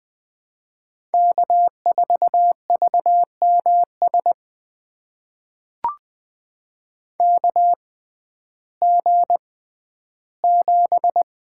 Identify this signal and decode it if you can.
Morse code, heard as a keyed tone, about 20 wpm, spaced as for 12 wpm, 700 Hz. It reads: K4VMS KG7
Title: USA Call Signs - 5 Characters - 6x Character Spacing - 20wpm